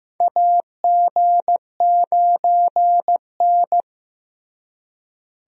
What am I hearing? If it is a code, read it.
Morse code, heard as a keyed tone, about 15 wpm, 700 Hz. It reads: AG9N
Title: USA Call Signs - 4 Characters - Repeated 3x - 15wpm